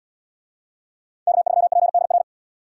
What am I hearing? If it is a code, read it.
Morse code, heard as a keyed tone, about 77 wpm, 700 Hz. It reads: K3PDR